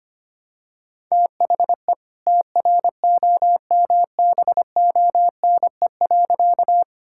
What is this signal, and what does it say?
Morse code, heard as a keyed tone, about 25 wpm, 700 Hz. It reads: THE TROMBONE.
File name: practice / 25wpm / All-Age Appropriate Jokes - 25wpm